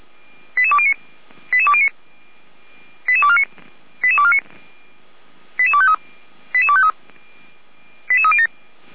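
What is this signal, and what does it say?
Five-tone selective call (ZVEI), heard as quick runs of five tones, tones of about 70 ms. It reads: 88209 88209 88249 88249 88253 88253 88307
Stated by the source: ZVEI five-tone selective call 88392 received on an SDR, logged by the Orlamünde fire brigade (feuerwehr-orlamuende).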